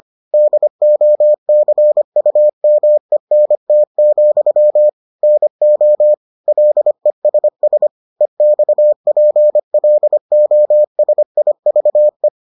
Morse code, heard as a keyed tone, about 25 wpm, 600 Hz. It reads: DOCUMENT, NO LESS EXPLOSIVE